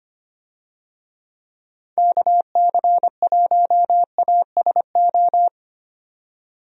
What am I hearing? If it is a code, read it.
Morse code, heard as a keyed tone, about 25 wpm, 700 Hz. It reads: KC1ASO
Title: USA Call Signs - 6 Characters - 25wpm